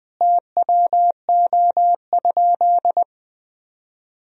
Morse code, heard as a keyed tone, about 20 wpm, 700 Hz. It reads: TWO?
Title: Sentences from Top 1000 Words - Review Part 1 - 20wpm